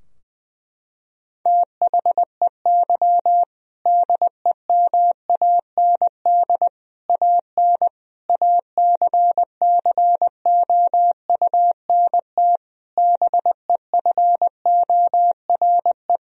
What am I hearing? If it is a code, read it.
Morse code, heard as a keyed tone, about 20 wpm, 700 Hz. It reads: THEY DEMAND AN ACCOUNT BEFORE
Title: Sentences from Top 1000 Words - Review Part 4 - 20wpm